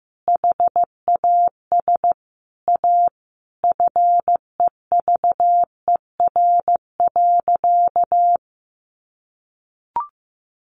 Morse code, heard as a keyed tone, about 15 wpm, 700 Hz. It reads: HAS A FEVER.
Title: Easy Phrases 15wpm